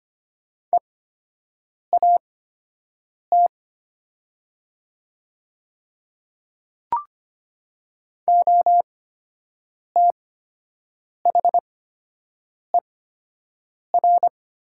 Morse code, heard as a keyed tone, about 25 wpm, spaced as for 7 wpm, 700 Hz. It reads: EAT OTHER
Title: 8x Character Spacing - Top 300 Words - 25wpm